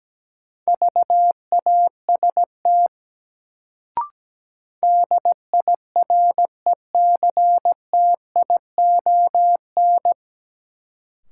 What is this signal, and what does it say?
Morse code, heard as a keyed tone, about 17 wpm, 700 Hz. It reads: VAST DIRECTION
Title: Sets of 4 Words - New Word Drill - 17wpm